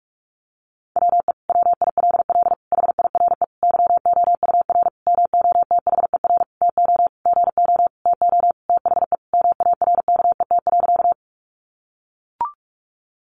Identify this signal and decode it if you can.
Morse code, heard as a keyed tone, about 44 wpm, 700 Hz. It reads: WE WILL HIRE YOUR MOTHER TO GO TO THE MARKET.
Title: Sentences from Top 1400 Words 45wpm